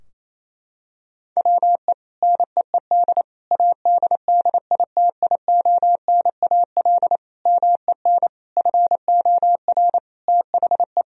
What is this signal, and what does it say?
Morse code, heard as a keyed tone, about 28 wpm, 700 Hz. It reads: WE NEED ADDITIONAL MEN FOR THE
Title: Sentences from Top 1000 Words - Review Part 4 - 28wpm